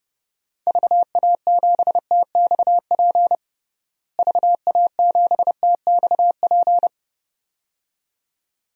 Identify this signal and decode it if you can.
Morse code, heard as a keyed tone, about 32 wpm, 700 Hz. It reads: VA7TXP VA7TXP
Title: DXCC-100 Call Signs - 6 Characters - Repeated 2x - 30wpm